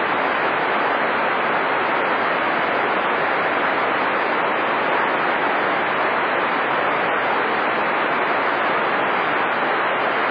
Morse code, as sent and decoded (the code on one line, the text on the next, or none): none